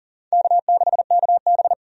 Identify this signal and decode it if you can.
Morse code, heard as a keyed tone, about 40 wpm, 700 Hz. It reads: K6KB
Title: USA Call Signs - 4 Characters - 40wpm